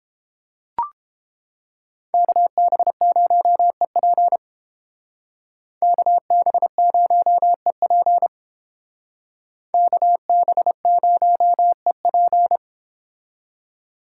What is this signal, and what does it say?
Morse code, heard as a keyed tone, about 28 wpm, 700 Hz. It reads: KB0EP KB0EP KB0EP